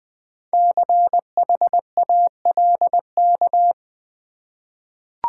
Morse code, heard as a keyed tone, about 20 wpm, 690 Hz. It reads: CHALK
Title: Easy Words with Hints - 20wpm